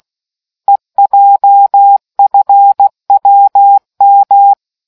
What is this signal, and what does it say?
Morse code, heard as a keyed tone, about 16 wpm, 800 Hz. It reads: EJFWM